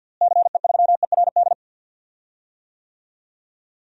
Morse code, heard as a keyed tone, about 46 wpm, 700 Hz. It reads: CE3ERD
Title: DXCC-100 Call Signs - 6 Characters - Repeated 2x - 50wpm